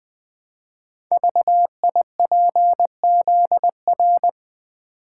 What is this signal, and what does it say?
Morse code, heard as a keyed tone, about 20 wpm, 700 Hz. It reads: VIPZR
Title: Code Groups - 5 Letters - 20wpm